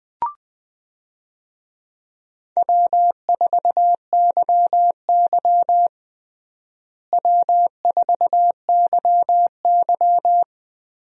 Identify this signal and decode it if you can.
Morse code, heard as a keyed tone, about 20 wpm, 700 Hz. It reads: W4YY W4YY
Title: USA Call Signs - 4 Characters - Repeated 3x - 20wpm